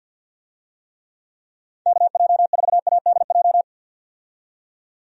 Morse code, heard as a keyed tone, about 50 wpm, 700 Hz. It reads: KJ4ADJ
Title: USA Call Signs - 6 Characters - 50wpm